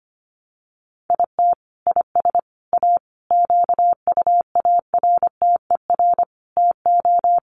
Morse code, heard as a keyed tone, about 25 wpm, 700 Hz. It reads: IT IS A QUARTER TO
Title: Sentences from Top 1300 Words 25wpm